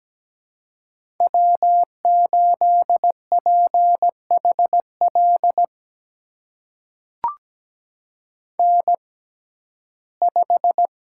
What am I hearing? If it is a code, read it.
Morse code, heard as a keyed tone, about 17 wpm, spaced as for 12 wpm, 700 Hz. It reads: W8PHL N5
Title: USA Call Signs - 5 Characters - 6x Character Spacing - 17wpm